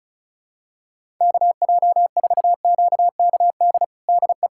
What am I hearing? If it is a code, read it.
Morse code, heard as a keyed tone, about 36 wpm, 700 Hz. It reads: KJ4QKD DE
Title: USA Callsign DE Callsign - 6 Characters - 35wpm